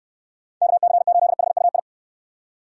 Morse code, heard as a keyed tone, about 69 wpm, 700 Hz. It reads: KC8HUI